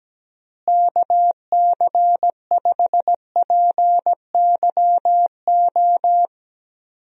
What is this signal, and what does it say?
Morse code, heard as a keyed tone, about 17 wpm, 700 Hz. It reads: KC5PYO
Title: USA Call Signs - 6 Characters - Repeated 3x - 17wpm